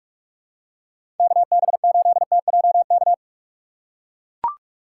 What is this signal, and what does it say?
Morse code, heard as a keyed tone, about 43 wpm, 700 Hz. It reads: KB8TJK
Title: USA Call Signs - 2x Character Spacing - 45wpm